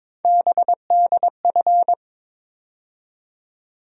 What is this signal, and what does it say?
Morse code, heard as a keyed tone, about 22 wpm, 700 Hz. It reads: BDF
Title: USA Call Sign Postfixes - 3 Letters 22wpm